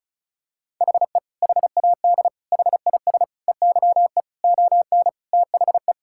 Morse code, heard as a keyed tone, about 36 wpm, 700 Hz. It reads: HE HAD HIS EYE ON THE
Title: Sentences from Top 1000 Words - Review Part 5 - 35wpm